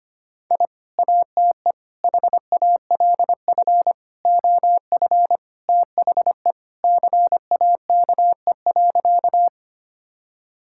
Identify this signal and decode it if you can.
Morse code, heard as a keyed tone, about 25 wpm, 700 Hz. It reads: I ATE HALF OF THE CAKE.
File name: practice / 25wpm / Sentences with 3-Word Phrases - 25wpm